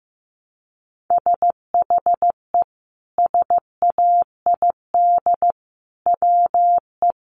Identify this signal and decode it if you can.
Morse code, heard as a keyed tone, about 15 wpm, 700 Hz. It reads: SHE SAID WE
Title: Sentences from Top 1300 Words 15wpm